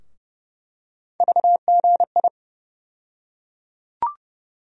Morse code, heard as a keyed tone, about 30 wpm, 700 Hz. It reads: VGI
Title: USA Call Sign Postfixes - 3 Letters 30wpm